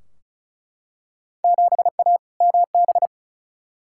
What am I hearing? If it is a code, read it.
Morse code, heard as a keyed tone, about 34 wpm, 700 Hz. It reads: 7A MB
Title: Field Day Exchanges Encore - 35wpm